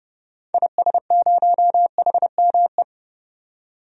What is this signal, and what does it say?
Morse code, heard as a keyed tone, about 30 wpm, 700 Hz. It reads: IS0HME
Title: DXCC-100 Call Signs - 6 Characters - 30wpm